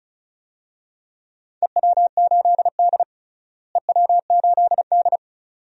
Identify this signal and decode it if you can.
Morse code, heard as a keyed tone, about 36 wpm, 700 Hz. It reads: EW8D EW8D
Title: DXCC-100 Call Signs - 4 Characters - Repeated 2x - 35wpm